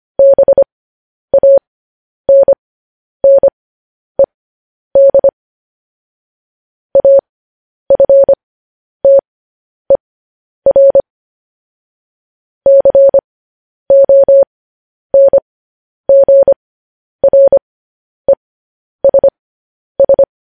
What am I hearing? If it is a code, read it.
Morse code, heard as a keyed tone, about 25 wpm, spaced as for 10 wpm, 550 Hz. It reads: BANNED AFTER CONGRESS